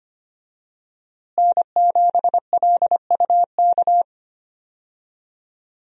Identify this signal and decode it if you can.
Morse code, heard as a keyed tone, about 25 wpm, 700 Hz. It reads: N7LUK
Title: USA Call Signs - 5 Characters - Repeated 3x - 25wpm